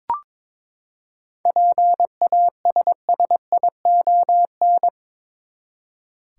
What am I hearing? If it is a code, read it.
Morse code, heard as a keyed tone, about 22 wpm, 700 Hz. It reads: PASSION
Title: Seven Letter Words - 22wpm